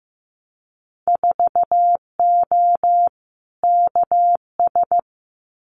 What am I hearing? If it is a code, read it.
Morse code, heard as a keyed tone, about 15 wpm, 700 Hz. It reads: 4O KS
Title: Winter Field Day Exchanges Encore 15wpm